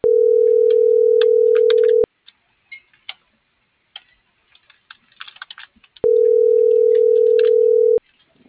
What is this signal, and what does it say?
Ringback tone (North America), 440 + 480 Hz.